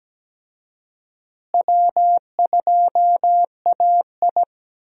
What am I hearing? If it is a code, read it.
Morse code, heard as a keyed tone, about 17 wpm, 700 Hz. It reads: W2AI
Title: USA Call Signs - 4 Characters - Repeated 3x - 17wpm